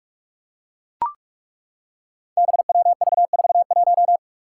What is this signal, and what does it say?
Morse code, heard as a keyed tone, about 41 wpm, 700 Hz. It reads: BWV41
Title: Five Challenging Letter-Number Combinations 45wpm